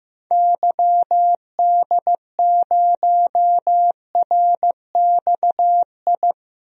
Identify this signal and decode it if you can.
Morse code, heard as a keyed tone, about 15 wpm, 700 Hz. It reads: YD0RXI
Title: DXCC-100 Call Signs - 6 Characters - Repeated 2x - 15wpm